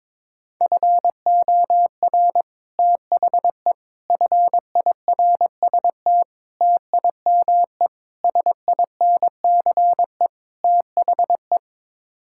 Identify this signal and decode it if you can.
Morse code, heard as a keyed tone, about 21 wpm, 700 Hz. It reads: FOR THE FIRST TIME SINCE THE
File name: practice / 22wpm / Sets of 6 Words - 22wpm